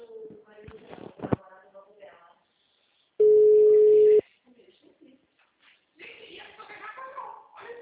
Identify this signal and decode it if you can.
Ringback tone (Europe, 425 Hz), with speech.